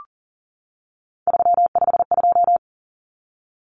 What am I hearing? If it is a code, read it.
Morse code, heard as a keyed tone, about 40 wpm, 700 Hz. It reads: 352
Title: Numbers - 3-Digit 40wpm